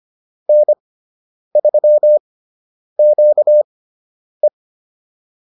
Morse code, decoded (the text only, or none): N3QE